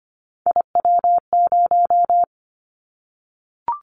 Morse code, heard as a keyed tone, about 25 wpm, 700 Hz. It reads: IW0
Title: Callsign Prefixes for Top 25 DXCC - 25wpm